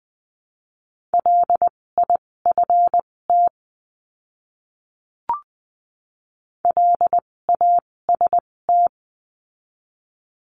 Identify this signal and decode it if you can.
Morse code, heard as a keyed tone, about 20 wpm, spaced as for 16 wpm, 700 Hz. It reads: LIFT LAST
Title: 256 Lesson 42a - Words 12-20wpm